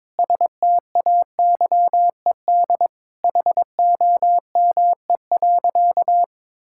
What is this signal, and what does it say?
Morse code, heard as a keyed tone, about 22 wpm, 700 Hz. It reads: STAYED HOME.
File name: practice / 22wpm / Sentences with 4-Word Phrases - 22wpm